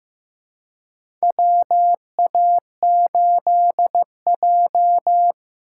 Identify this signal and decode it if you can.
Morse code, heard as a keyed tone, about 15 wpm, 700 Hz. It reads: WA8J
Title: USA Call Signs - 4x Character Spacing - 15wpm